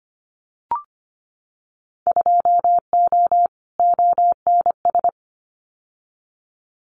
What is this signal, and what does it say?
Morse code, heard as a keyed tone, about 25 wpm, 700 Hz. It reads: 2O ONS
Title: Winter Field Day Exchanges Encore 25wpm